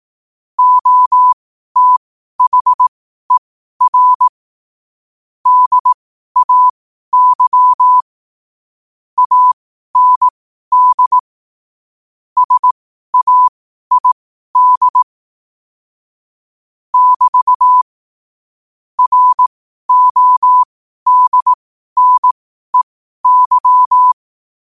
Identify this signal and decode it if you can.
Morse code, heard as a keyed tone, about 18 wpm, spaced as for 12 wpm, 1000 Hz. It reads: OTHER DAY AND SAID <BT> RODNEY
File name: qotdfiles / 12wpm